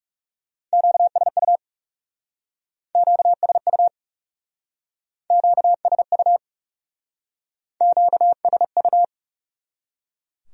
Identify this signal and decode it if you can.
Morse code, heard as a keyed tone, about 39 wpm, 700 Hz. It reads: QSU QSU QSU QSU